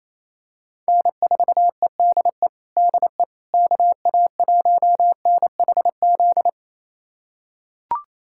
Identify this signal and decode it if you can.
Morse code, heard as a keyed tone, about 28 wpm, 700 Hz. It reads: N4EDE DE KA1NHZ
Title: USA Callsign DE Callsign 28wpm